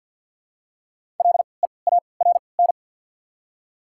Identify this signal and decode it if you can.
Morse code, heard as a keyed tone, about 54 wpm, spaced as for 29 wpm, 700 Hz. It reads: LEARN
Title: 3x Character Spacing - Top 500 Words - 50wpm